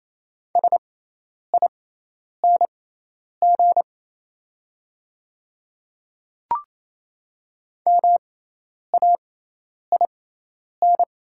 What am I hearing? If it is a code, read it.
Morse code, heard as a keyed tone, about 28 wpm, spaced as for 10 wpm, 690 Hz. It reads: SING MAIN
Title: 6x Character Spacing - Top 500 Words - 28wpm